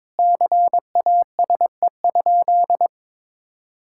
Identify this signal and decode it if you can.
Morse code, heard as a keyed tone, about 22 wpm, 700 Hz. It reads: CASE?
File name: practice / 22wpm / Sentences with 4-Word Phrases - 22wpm